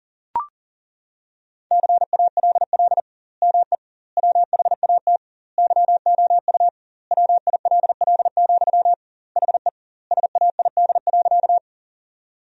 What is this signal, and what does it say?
Morse code, heard as a keyed tone, about 40 wpm, 700 Hz. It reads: CALL ME WHAT YOU WILL, HE SAID.
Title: 2x Word-Spacing - Sentences from Top 200 Words - 40wpm